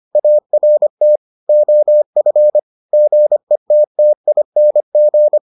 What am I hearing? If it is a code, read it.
Morse code, heard as a keyed tone, about 25 wpm, 600 Hz. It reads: ART OF GETTING